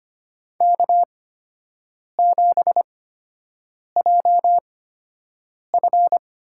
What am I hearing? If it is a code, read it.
Morse code, heard as a keyed tone, about 25 wpm, spaced as for 7 wpm, 700 Hz. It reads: K7JF
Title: USA Call Signs - 4 Characters - 8x Character Spacing - 25wpm